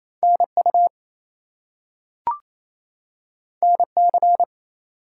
Morse code, heard as a keyed tone, about 28 wpm, 700 Hz. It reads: NU NC
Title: US States and Canadian Provinces 28wpm